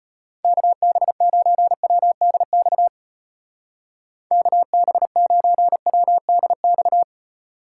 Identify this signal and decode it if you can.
Morse code, heard as a keyed tone, about 32 wpm, 700 Hz. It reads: KB9WDX KB9WDX